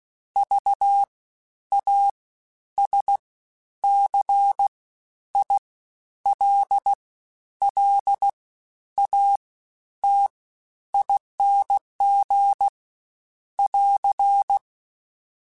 Morse code, heard as a keyed tone, about 16 wpm, spaced as for 10 wpm, 800 Hz. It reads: VASCILLATING <AR>
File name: Special INGs-mp3s / Special INGs-mp3s / Fast ing @15